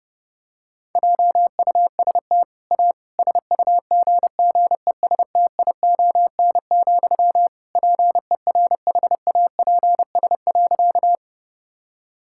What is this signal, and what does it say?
Morse code, heard as a keyed tone, about 30 wpm, 700 Hz. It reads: JUST A SUGGESTION, PERHAPS.
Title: Sentences from Top 1600 Words 30wpm